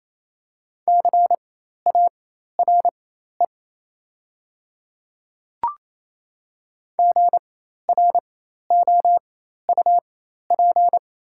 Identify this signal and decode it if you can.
Morse code, heard as a keyed tone, about 28 wpm, spaced as for 13 wpm, 700 Hz. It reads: CARE GROUP
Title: 4x Character Spacing - Top 300 Words - 28wpm